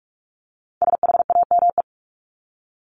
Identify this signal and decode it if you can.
Morse code, heard as a keyed tone, about 44 wpm, 700 Hz. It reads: SHAME
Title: Top 1751-2000 Words 45wpm